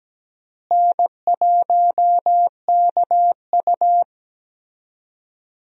Morse code, heard as a keyed tone, about 17 wpm, 700 Hz. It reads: N1KU
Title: USA Call Signs - 4 Characters - Repeated 2x - 17wpm